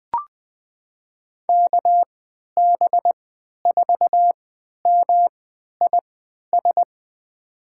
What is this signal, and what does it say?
Morse code, heard as a keyed tone, about 20 wpm, spaced as for 11 wpm, 700 Hz. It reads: KB4MIS
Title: USA Call Signs - 6 Characters - 3x Character Spacing - 20wpm